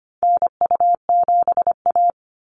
Morse code, heard as a keyed tone, about 25 wpm, 700 Hz. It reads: NU7A
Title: NA SOTA Chaser Callsigns 25wpm